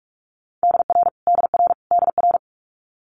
Morse code, heard as a keyed tone, about 46 wpm, 700 Hz. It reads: DR DR DR
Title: Multi-letter Phonemes Repeated 3x 45wpm